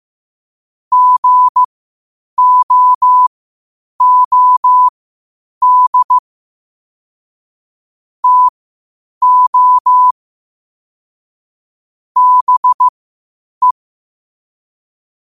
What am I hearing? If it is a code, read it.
Morse code, heard as a keyed tone, about 15 wpm, spaced as for 8 wpm, 1000 Hz. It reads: GOOD TO BE
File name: qotdfiles / 8wpm